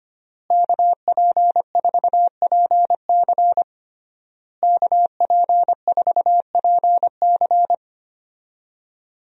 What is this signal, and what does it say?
Morse code, heard as a keyed tone, about 25 wpm, 700 Hz. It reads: KP4PC KP4PC
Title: DXCC-100 Call Signs - 5 Characters - Repeated 3x - 25wpm